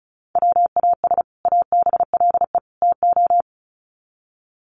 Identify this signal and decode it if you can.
Morse code, heard as a keyed tone, about 34 wpm, 700 Hz. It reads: WAS ABLE TO
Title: Sets of 3 Words 35wpm